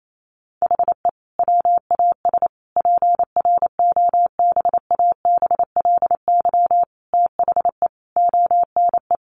Morse code, heard as a keyed tone, about 28 wpm, 700 Hz. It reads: HE WAS PROBABLY THE ONE
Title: Sentences from Top 1300 Words 28wpm